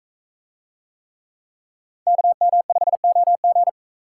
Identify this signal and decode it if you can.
Morse code, heard as a keyed tone, about 41 wpm, 700 Hz. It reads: KM5OG